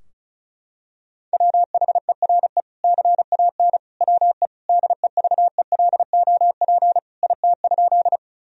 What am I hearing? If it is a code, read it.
Morse code, heard as a keyed tone, about 34 wpm, 700 Hz. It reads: WHERE CAN WE DEVELOP IT?
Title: Sentences from Top 1000 Words - Review Part 2 - 35wpm